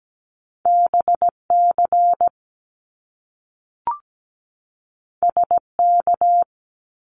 Morse code, heard as a keyed tone, about 17 wpm, 700 Hz. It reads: BC SK
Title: Canadian Provinces and Territories 17wpm